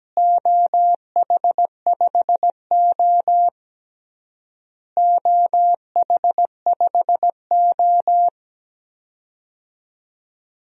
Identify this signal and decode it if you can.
Morse code, heard as a keyed tone, about 17 wpm, 700 Hz. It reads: OH5O OH5O
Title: DXCC-100 Call Signs - 4 Characters - Repeated 3x - 17wpm